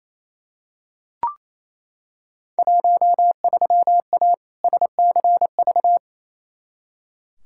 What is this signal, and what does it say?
Morse code, heard as a keyed tone, about 28 wpm, 700 Hz. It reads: 13A SCV
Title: Field Day Exchanges Encore - 28wpm